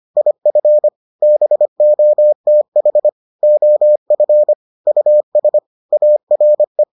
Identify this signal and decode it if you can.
Morse code, heard as a keyed tone, about 25 wpm, 600 Hz. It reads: IF BOTH OF US ARE